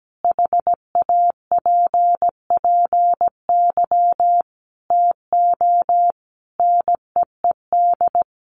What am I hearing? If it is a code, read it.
Morse code, heard as a keyed tone, about 17 wpm, 700 Hz. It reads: HAPPY TO NEED
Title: Sentences from Top 1000 Words 17wpm